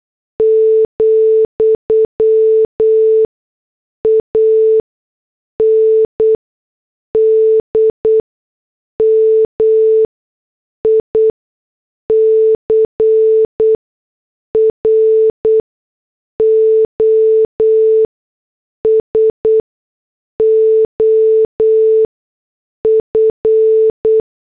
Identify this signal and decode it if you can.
Morse code, heard as a keyed tone, about 8 wpm, spaced as for 6 wpm, 440 Hz. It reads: , AND MICROSOF